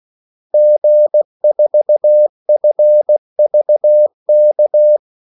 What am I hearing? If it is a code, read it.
Morse code, heard as a keyed tone, about 16 wpm, 600 Hz. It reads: G4FVK
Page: Kaldesignaler 16 wpm